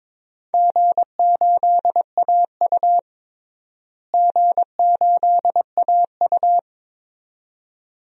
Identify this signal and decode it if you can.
Morse code, heard as a keyed tone, about 22 wpm, 700 Hz. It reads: G8AU G8AU
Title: DXCC-100 Call Signs - 4 Characters - Repeated 2x - 22wpm